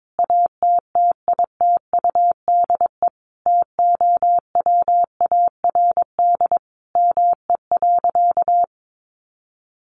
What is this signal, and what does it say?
Morse code, heard as a keyed tone, about 22 wpm, 700 Hz. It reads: ATTITUDE TOWARD ME.
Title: Sentences from Top 1300 Words 22wpm